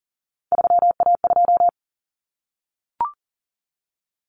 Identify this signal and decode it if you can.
Morse code, heard as a keyed tone, about 40 wpm, 700 Hz. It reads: 3A2
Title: Common Cut Numbers - 3-Digit 40wpm